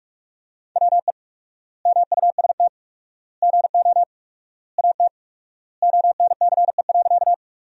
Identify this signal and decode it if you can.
Morse code, heard as a keyed tone, about 41 wpm, 700 Hz. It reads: WE MUST GO AT ONCE.
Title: 4x Word-Spacing - Sentences from Top 300 Words - 45wpm